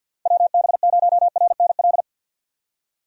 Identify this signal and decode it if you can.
Morse code, heard as a keyed tone, about 51 wpm, 700 Hz. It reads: WB0RNL
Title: USA Call Signs - 6 Characters - 50wpm